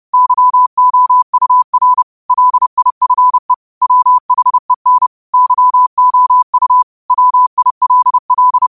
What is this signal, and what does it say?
Morse code, heard as a keyed tone, about 30 wpm, 1000 Hz. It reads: YOUR LIFE WHEN YOU WILL
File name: qotdfiles / 30wpm